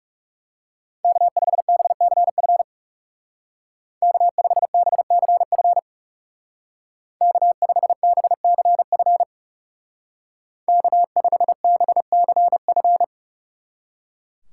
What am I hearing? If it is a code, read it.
Morse code, heard as a keyed tone, about 36 wpm, 700 Hz. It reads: K5BCF K5BCF K5BCF K5BCF
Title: Speed Racing - USA Call Signs 30wpm